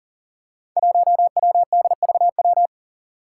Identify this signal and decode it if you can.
Morse code, heard as a keyed tone, about 40 wpm, 690 Hz. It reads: 1WDVW
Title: Five Challenging Letter-Number Combinations 40wpm